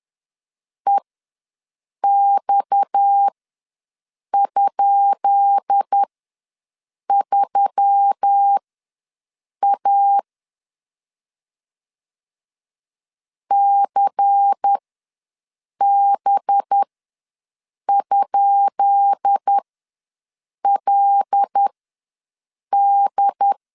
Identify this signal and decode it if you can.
Morse code, heard as a keyed tone, about 10 wpm, spaced as for 6 wpm, 800 Hz. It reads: EX?3A CB?LD
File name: LEZIONI / 10 SIGNS / 10B INT ws 10wpm 4134